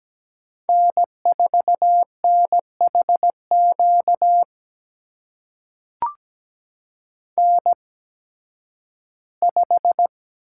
Morse code, heard as a keyed tone, about 17 wpm, spaced as for 11 wpm, 700 Hz. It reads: N4NHQ N5
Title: USA Call Signs - 8x Character Spacing - 17wpm